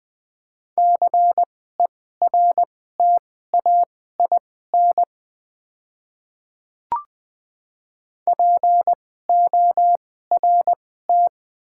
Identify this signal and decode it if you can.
Morse code, heard as a keyed tone, about 20 wpm, spaced as for 15 wpm, 700 Hz. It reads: CERTAIN PORT
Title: 2x Character Spacing - Top 500 Words - 20wpm